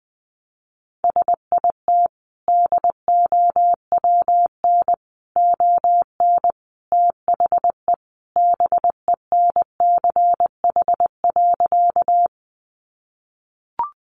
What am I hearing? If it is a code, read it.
Morse code, heard as a keyed tone, about 20 wpm, 700 Hz. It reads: SIT DOWN ON THE BENCH.
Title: Sentences from Top 1400 Words 20wpm